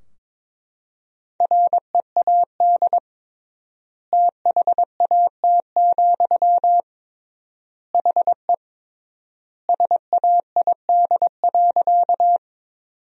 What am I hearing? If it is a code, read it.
Morse code, heard as a keyed tone, about 22 wpm, 700 Hz. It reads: READ THAT, HE SAID.
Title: 3x Word-Spacing - Sentences from Top 200 Words - 22wpm